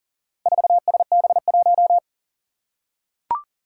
Morse code, heard as a keyed tone, about 37 wpm, 700 Hz. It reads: 4SB1
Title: Four Challenging Letter-Number Combinations 40wpm